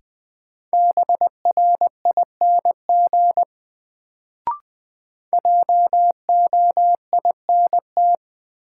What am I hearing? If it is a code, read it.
Morse code, heard as a keyed tone, about 20 wpm, 700 Hz. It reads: BRING JOINT
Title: Sets of 5 Words Encore - New Word Drill - 20wpm